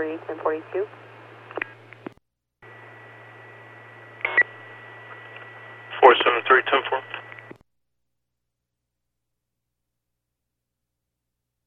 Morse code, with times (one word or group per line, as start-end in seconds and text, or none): none